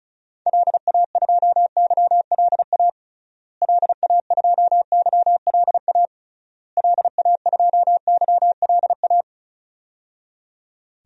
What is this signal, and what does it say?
Morse code, heard as a keyed tone, about 36 wpm, 700 Hz. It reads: LA2YLA LA2YLA LA2YLA
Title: DXCC-100 Call Signs - 6 Characters - Repeated 3x - 35wpm